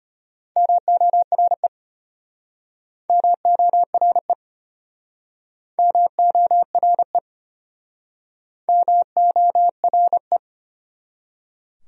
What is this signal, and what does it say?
Morse code, heard as a keyed tone, about 35 wpm, 700 Hz. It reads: MORE MORE MORE MORE